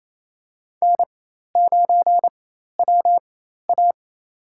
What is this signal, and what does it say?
Morse code, heard as a keyed tone, about 28 wpm, spaced as for 13 wpm, 700 Hz. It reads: N9WA
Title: USA Call Signs - 4x Character Spacing - 28wpm